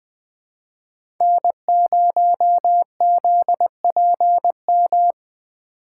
Morse code, heard as a keyed tone, about 20 wpm, 700 Hz. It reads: N0ZPM
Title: Speed Racing - USA 1x3 Call Signs 17wpm